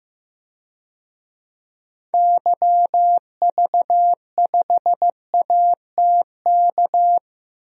Morse code, heard as a keyed tone, about 15 wpm, 700 Hz. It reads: YV5ATK
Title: DXCC-100 Call Signs - 6 Characters - Repeated 3x - 15wpm